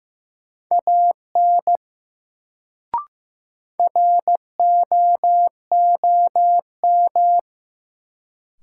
Morse code, heard as a keyed tone, about 16 wpm, 700 Hz. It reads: AN ROOM
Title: Sentences from Top 300 Words - New Word Drill - 15wpm